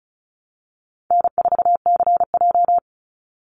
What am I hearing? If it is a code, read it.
Morse code, heard as a keyed tone, about 36 wpm, 700 Hz. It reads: N4CJ
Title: Call Signs - 35wpm